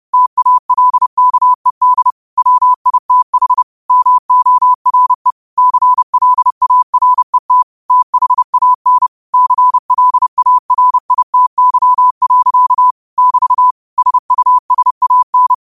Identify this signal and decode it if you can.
Morse code, heard as a keyed tone, about 30 wpm, 1000 Hz. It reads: TALKED WITH MORE CLARET THAN CLARITY. <BT> SUSAN
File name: qotdfiles / 30wpm